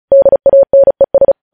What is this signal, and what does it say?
Morse code, heard as a keyed tone, about 36 wpm, 550 Hz. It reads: DANES
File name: podcast / News headlines-1x-35x35